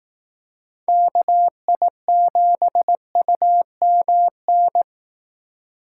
Morse code, heard as a keyed tone, about 19 wpm, 700 Hz. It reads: KI7UMN
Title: Speed Racing - USA 2x3 Call Signs 15wpm